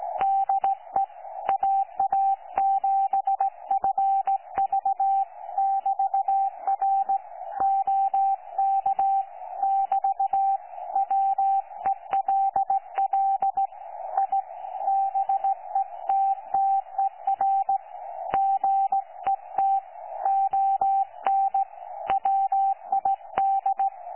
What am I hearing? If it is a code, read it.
Morse code, heard as a keyed tone, about 16 wpm, 780 Hz. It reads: DE AA7FV<BT>R OK <BT> WELL I BETTER GET ON WID
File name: 12 39Z 18088.2kHz utah aa7fv k9tjl